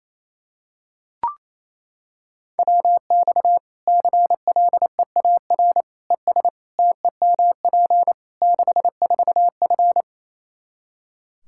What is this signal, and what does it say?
Morse code, heard as a keyed tone, about 28 wpm, 700 Hz. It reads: WX CLEAR ES TEMP 64F
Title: K1USN SST QSO Practice 28wpm